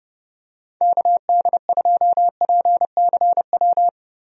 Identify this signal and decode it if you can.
Morse code, heard as a keyed tone, about 30 wpm, 700 Hz. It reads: KD2PCW
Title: USA Call Signs - 6 Characters - Repeated 2x - 30wpm